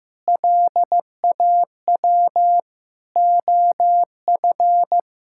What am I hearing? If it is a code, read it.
Morse code, heard as a keyed tone, about 15 wpm, 700 Hz. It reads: LAW OF
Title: Sentences from Top 1000 Words - Review Part 6 - 15wpm